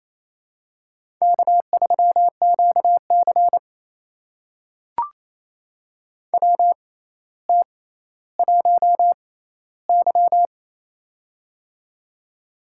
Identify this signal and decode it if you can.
Morse code, heard as a keyed tone, about 28 wpm, spaced as for 14 wpm, 700 Hz. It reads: K3QC WT1Y
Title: USA Call Signs - 4 Characters - 6x Character Spacing - 28wpm